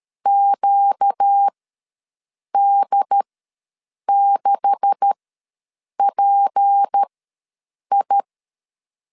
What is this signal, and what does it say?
Morse code, heard as a keyed tone, about 13 wpm, spaced as for 7 wpm, 800 Hz. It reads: QD6PI
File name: LEZIONI / 9 286 ws 12wpm 3351